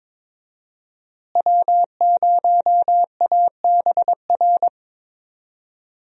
Morse code, heard as a keyed tone, about 22 wpm, 690 Hz. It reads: W0ABR
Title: USA Call Signs - 5 Characters - Repeated 3x - 22wpm